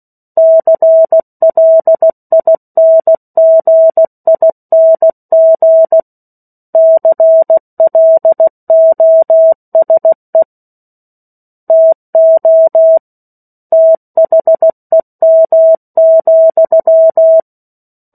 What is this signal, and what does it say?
Morse code, heard as a keyed tone, about 16 wpm, 650 Hz. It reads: CLINGING CLOSE TO THEM,
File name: 03-Chapter-03 16 WPM